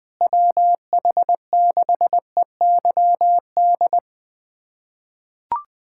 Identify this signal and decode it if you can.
Morse code, heard as a keyed tone, about 21 wpm, 700 Hz. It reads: WH6EYD
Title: USA Callsign DE Callsign 20wpm